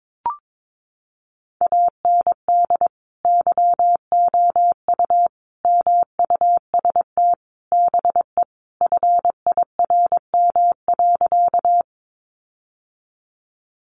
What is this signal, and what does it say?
Morse code, heard as a keyed tone, about 22 wpm, 700 Hz. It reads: AND YOU MUST BE FIRM.
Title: Sentences from Top 1400 Words 22wpm